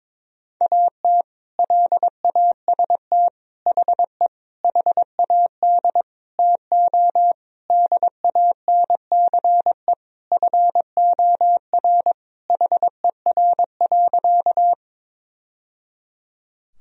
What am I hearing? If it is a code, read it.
Morse code, heard as a keyed tone, about 22 wpm, 700 Hz. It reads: AT LAST HE HAD TO DANCE FOR HER.
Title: Sentences from Top 1000 Words - Review Part 4 - 22wpm